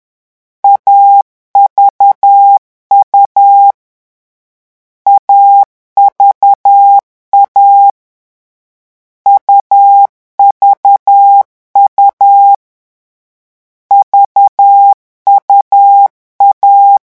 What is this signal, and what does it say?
Morse code, heard as a keyed tone, about 10 wpm, 800 Hz. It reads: AVU AVA UVU VUA
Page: Leçon 3 - vitesse 10 mots minute